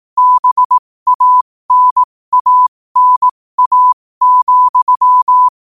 Morse code, heard as a keyed tone, about 18 wpm, spaced as for 15 wpm, 1000 Hz. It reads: BANANA,
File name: qotdfiles / 15wpm